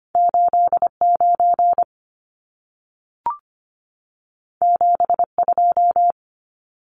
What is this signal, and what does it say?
Morse code, heard as a keyed tone, about 25 wpm, 700 Hz. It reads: 89 72
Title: Numbers - 2-Digit 25wpm